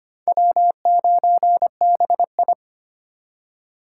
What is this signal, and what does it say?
Morse code, heard as a keyed tone, about 25 wpm, 700 Hz. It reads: W9BI
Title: USA Call Signs - 4 Characters - 4x Character Spacing - 25wpm